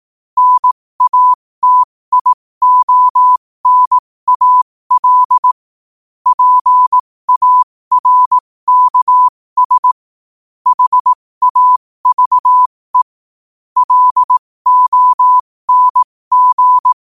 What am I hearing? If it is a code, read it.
Morse code, heard as a keyed tone, about 18 wpm, spaced as for 15 wpm, 1000 Hz. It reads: NATIONAL PARKS HAVE LONG
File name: qotdfiles / 15wpm